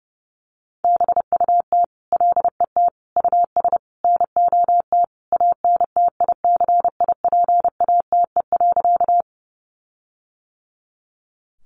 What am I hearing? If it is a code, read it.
Morse code, heard as a keyed tone, about 30 wpm, 700 Hz. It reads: BUT LET US NOT ANTICIPATE.
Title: Sentences from Top 1500 Words 30wpm